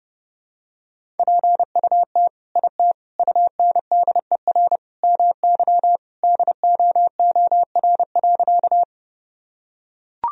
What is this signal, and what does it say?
Morse code, heard as a keyed tone, about 32 wpm, 700 Hz. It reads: PUT IT UNDER MY DOOR.
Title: Sentences from Top 1000 Words - Review Part 1 - 30wpm